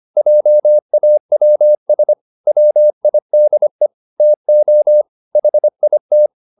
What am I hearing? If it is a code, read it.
Morse code, heard as a keyed tone, about 25 wpm, 600 Hz. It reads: JAWS WIDE TO HIT